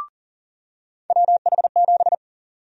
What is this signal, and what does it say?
Morse code, heard as a keyed tone, about 37 wpm, 700 Hz. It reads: WH7
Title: USA Call Sign Prefixes - 40wpm